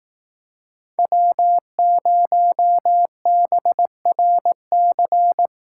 Morse code, heard as a keyed tone, about 18 wpm, 700 Hz. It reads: W0BRC
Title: Speed Racing - USA 1x3 Call Signs 15wpm